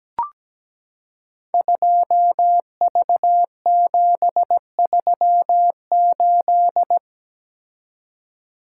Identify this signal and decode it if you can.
Morse code, heard as a keyed tone, about 17 wpm, 700 Hz. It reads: 2V738
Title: Code Groups - 5 Letter:Numbers - 17wpm